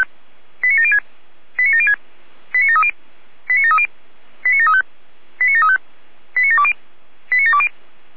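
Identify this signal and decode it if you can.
Five-tone selective call (ZVEI), heard as quick runs of five tones, tones of about 70 ms. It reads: 89706 89706 89730 89730 89735 89735 89720 89720
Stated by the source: ZVEI five-tone selective call 89752, received by SDR.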